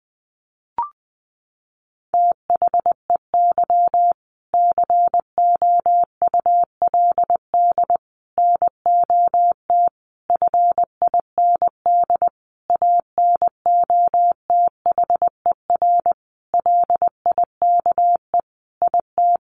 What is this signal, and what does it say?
Morse code, heard as a keyed tone, about 20 wpm, 700 Hz. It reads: THEY COULD NOT FIND ANOTHER LIKE IT
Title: Sentences from Top 1400 Words 20wpm